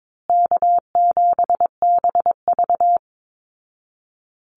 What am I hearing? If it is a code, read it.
Morse code, heard as a keyed tone, about 22 wpm, 700 Hz. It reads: K7BV
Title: Easy Call Signs Repeated 2x - 22wpm